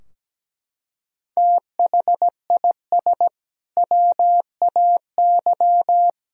Morse code, heard as a keyed tone, about 17 wpm, 700 Hz. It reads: THIS WAY
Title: Sentences from Top 1000 Words - Review Part 2 - 17wpm